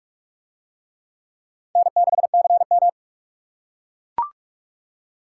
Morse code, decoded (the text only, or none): N6CM